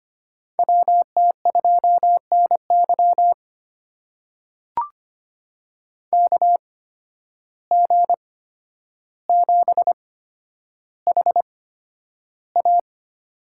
Morse code, heard as a keyed tone, about 25 wpm, spaced as for 11 wpm, 700 Hz. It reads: WT2NY KG7HA
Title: USA Call Signs - 5 Characters - 8x Character Spacing - 25wpm